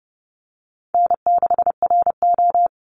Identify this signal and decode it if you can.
Morse code, heard as a keyed tone, about 30 wpm, 700 Hz. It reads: N6RO
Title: Easy Call Signs - 30wpm